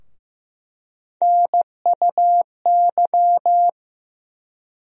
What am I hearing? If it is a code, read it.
Morse code, heard as a keyed tone, about 15 wpm, 700 Hz. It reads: NUY
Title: USA Call Sign Postfixes - 3 Letters 15wpm